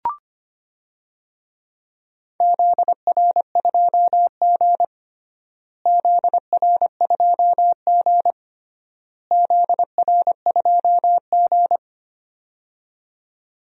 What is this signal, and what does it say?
Morse code, heard as a keyed tone, about 25 wpm, 700 Hz. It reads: ZR2G ZR2G ZR2G
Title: DXCC-100 Call Signs - 4 Characters - Repeated 3x - 25wpm